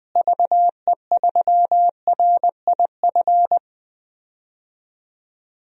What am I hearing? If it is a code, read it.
Morse code, heard as a keyed tone, about 20 wpm, 700 Hz. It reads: VE3RIF
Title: DXCC-100 Call Signs - 6 Characters - Repeated 3x - 20wpm